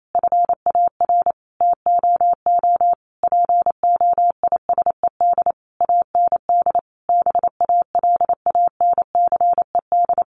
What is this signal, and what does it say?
Morse code, heard as a keyed tone, about 28 wpm, 700 Hz. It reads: FAR TOO POISED AND BALANCED
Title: A Sound Of Thunder 28wpm